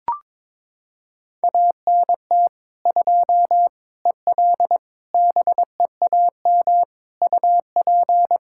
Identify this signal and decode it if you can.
Morse code, heard as a keyed tone, about 22 wpm, 700 Hz. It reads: ANT 2 EL BEAM UP
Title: Practice QSO - Exchange 2 - Info - 22wpm